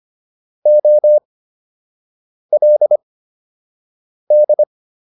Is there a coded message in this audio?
Yes: Morse code from 0.7 to 4.6 s, heard as a keyed tone, about 25 wpm, spaced as for 6 wpm, 600 Hz. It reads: OLD